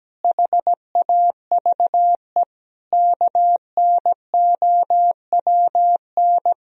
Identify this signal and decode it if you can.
Morse code, heard as a keyed tone, about 17 wpm, 700 Hz. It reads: HAVE KNOWN
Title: Sentences from Top 1000 Words - Review Part 3 - 17wpm